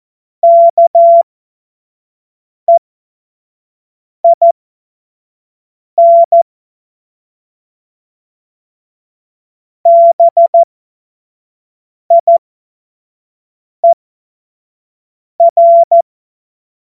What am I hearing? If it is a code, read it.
Morse code, heard as a keyed tone, about 14 wpm, spaced as for 5 wpm, 680 Hz. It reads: KEIN BIER
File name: Morsekurs / Witze / Witz 15. Juni 2021, 25 F70